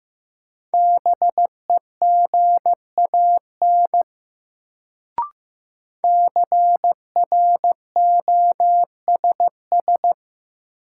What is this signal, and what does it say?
Morse code, heard as a keyed tone, about 15 wpm, 700 Hz. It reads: BEGAN CROSS
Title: Sentences from Top 300 Words - New Word Drill - 15wpm